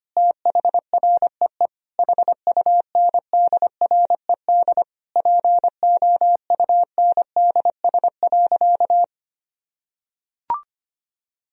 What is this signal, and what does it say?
Morse code, heard as a keyed tone, about 25 wpm, 700 Hz. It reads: THREE HUNDRED POUNDS.
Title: Sentences from Top 1000 Words - Review Part 4 - 25wpm